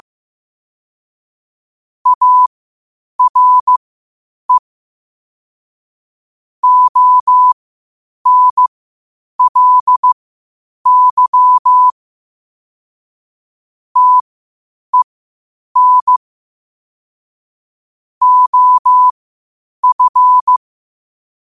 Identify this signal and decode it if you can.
Morse code, heard as a keyed tone, about 15 wpm, spaced as for 8 wpm, 1000 Hz. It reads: ARE ONLY TEN OF
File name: qotdfiles / 8wpm